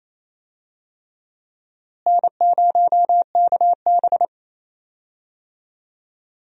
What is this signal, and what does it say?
Morse code, heard as a keyed tone, about 28 wpm, 700 Hz. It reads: N0KB
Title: USA Call Signs - 4 Characters - Repeated 3x - 28wpm